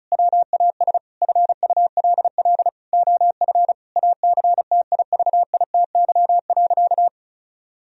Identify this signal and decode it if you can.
Morse code, heard as a keyed tone, about 34 wpm, 700 Hz. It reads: WAS FULL OF ACTIVITY.
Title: Sentences from Top 1000 Words - Review Part 4 - 35wpm